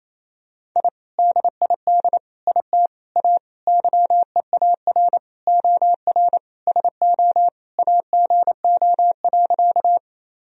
Morse code, heard as a keyed tone, about 27 wpm, 700 Hz. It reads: I DID IT A YEAR OR SO AGO.
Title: Sentences from Top 1000 Words - Review Part 3 - 28wpm